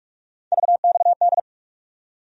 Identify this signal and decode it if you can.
Morse code, heard as a keyed tone, about 44 wpm, 700 Hz. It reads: VXD
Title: USA Call Sign Postfixes - 3 Letters 45wpm